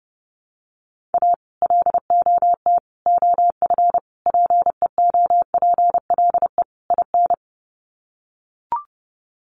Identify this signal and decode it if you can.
Morse code, heard as a keyed tone, about 30 wpm, 700 Hz. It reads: A LOT OF PEOPLE IN
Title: Sets of 5 Words 30wpm